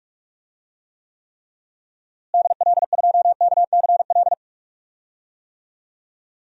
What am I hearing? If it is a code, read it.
Morse code, heard as a keyed tone, about 46 wpm, 700 Hz. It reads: DL2KCL
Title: DXCC-100 Call Signs - 6 Characters - Repeated 3x - 45wpm